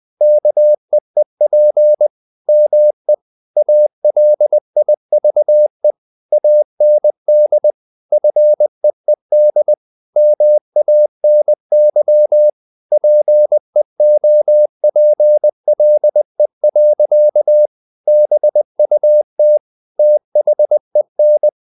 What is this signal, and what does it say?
Morse code, heard as a keyed tone, about 20 wpm, 600 Hz. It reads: KEEP ME ALIVE AND FEED MANY PEOPLE. BUT THEN
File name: the old man and the sea part 2 20wpm